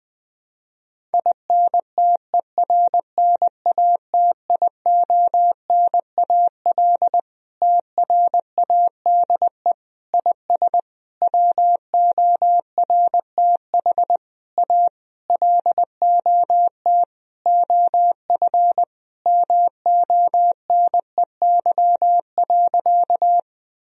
Morse code, heard as a keyed tone, about 20 wpm, 700 Hz. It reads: INTERNATIONAL TRADE IS WORTH A LOT OF MONEY.
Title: Sentences from Top 1000 Words - Review Part 4 - 20wpm